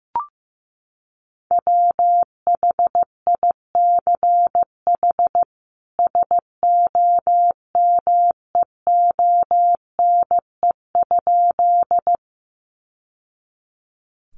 Morse code, heard as a keyed tone, about 15 wpm, 700 Hz. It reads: WHICH SOMEONE?
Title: Sentences from Top 600 Words 15wpm